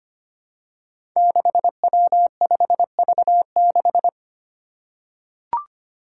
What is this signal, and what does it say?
Morse code, heard as a keyed tone, about 25 wpm, 700 Hz. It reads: 6W5V6
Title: Five Challenging Letter-Number Combinations 25wpm